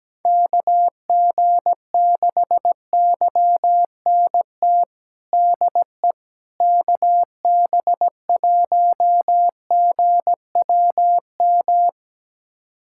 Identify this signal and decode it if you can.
Morse code, heard as a keyed tone, about 17 wpm, 700 Hz. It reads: KG6YNT DE KB1GWM
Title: USA Callsign DE Callsign - 6 Characters - 17wpm